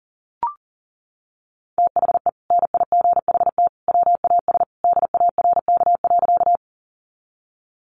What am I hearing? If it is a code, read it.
Morse code, heard as a keyed tone, about 40 wpm, 700 Hz. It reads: THE NIGHT WAS DARK.
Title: Sentences from Top 500 Words 40wpm